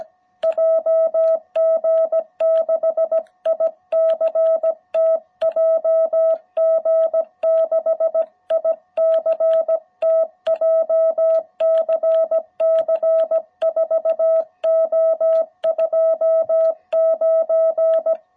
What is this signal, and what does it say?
Morse code, heard as a keyed tone, about 17 wpm, 660 Hz. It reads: JG6ICT JG6ICT JCC 4O29